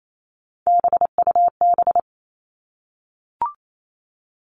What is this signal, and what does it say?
Morse code, heard as a keyed tone, about 28 wpm, 700 Hz. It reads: BUB
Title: All Cut Numbers - 3-Digit 28wpm